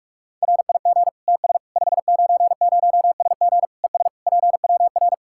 Morse code, heard as a keyed tone, about 46 wpm, 700 Hz. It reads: RIG TS 590SG ES PWR
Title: Practice QSO - Exchange 2 - Info - 45wpm